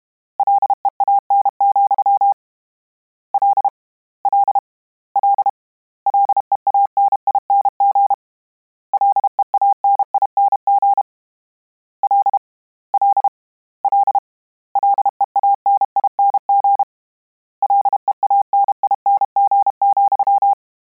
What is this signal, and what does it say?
Morse code, heard as a keyed tone, about 32 wpm, 800 Hz. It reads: LEAN, L L L LEANING LEANING L L L LEANING LEANING,
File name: Combinations / Combinations-mp3s / Combinations@30